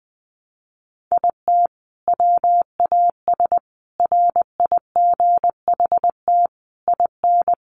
Morse code, heard as a keyed tone, about 20 wpm, 700 Hz. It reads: IT WAS RIGHT IN
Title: Sentences from Top 500 Words 20wpm